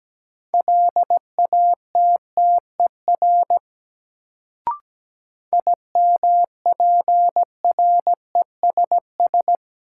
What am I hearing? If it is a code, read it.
Morse code, heard as a keyed tone, about 17 wpm, 700 Hz. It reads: LATTER IMPRESS
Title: Sentences from Top 1600 Words - New Word Drill Alt - 17wpm